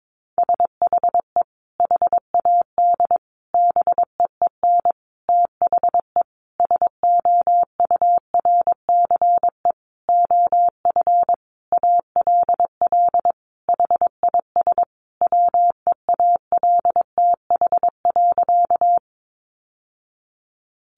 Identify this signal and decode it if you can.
Morse code, heard as a keyed tone, about 22 wpm, 700 Hz. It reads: SHE HAD BEEN THE SOURCE OF ALL HIS WEALTH.
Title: Sentences from Top 1200 Words 22wpm